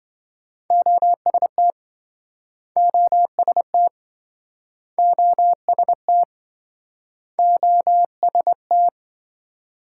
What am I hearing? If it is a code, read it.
Morse code, heard as a keyed tone, about 27 wpm, 700 Hz. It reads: OST OST OST OST